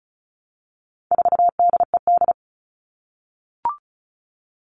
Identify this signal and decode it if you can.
Morse code, heard as a keyed tone, about 34 wpm, 700 Hz. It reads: 4DED
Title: All Cut Numbers - 4-Digit 35wpm